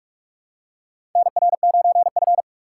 Morse code, heard as a keyed tone, about 44 wpm, 700 Hz. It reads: NR9F
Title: USA Call Signs - 4 Characters - 6x Character Spacing - 45wpm